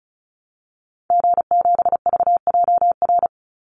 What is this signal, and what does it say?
Morse code, heard as a keyed tone, about 34 wpm, 700 Hz. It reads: G7VJR
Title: Call Signs Repeated 3x - 35wpm